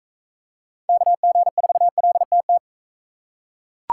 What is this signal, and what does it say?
Morse code, heard as a keyed tone, about 41 wpm, 700 Hz. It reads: KG4LTT